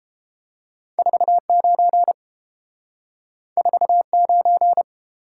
Morse code, heard as a keyed tone, about 30 wpm, 700 Hz. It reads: 49 49